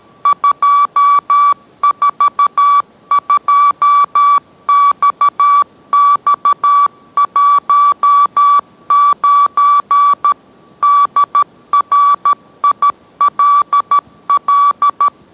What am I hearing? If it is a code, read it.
Morse code, heard as a keyed tone, about 13 wpm, 1150 Hz. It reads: 242XX19 DRILL